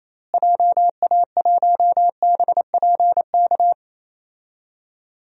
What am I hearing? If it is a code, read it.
Morse code, heard as a keyed tone, about 28 wpm, 700 Hz. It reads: JA1BPK
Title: DXCC-100 Call Signs - 6 Characters - Repeated 3x - 28wpm